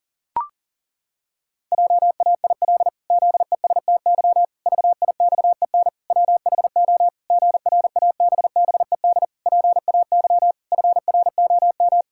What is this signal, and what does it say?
Morse code, heard as a keyed tone, about 40 wpm, 700 Hz. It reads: JAIL ZESTY VIXEN WHO GRABBED PAY FROM
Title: Pangram Challenge v2 - 40wpm